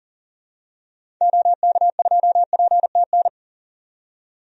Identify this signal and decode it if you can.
Morse code, heard as a keyed tone, about 40 wpm, 700 Hz. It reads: OK2PTN
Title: DXCC-100 Call Signs - 6 Characters - 40wpm